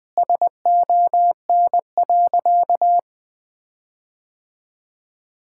Morse code, heard as a keyed tone, about 20 wpm, 700 Hz. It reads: SON.
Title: Sentences with 9-Word Phrases - 20wpm